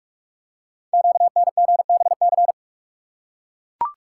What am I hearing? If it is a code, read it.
Morse code, heard as a keyed tone, about 44 wpm, 690 Hz. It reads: QNGBC
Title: Code Groups - 5 Letters - 45wpm